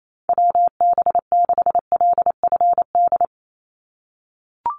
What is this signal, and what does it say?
Morse code, heard as a keyed tone, about 28 wpm, 700 Hz. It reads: WB6LFD
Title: USA Call Signs 28wpm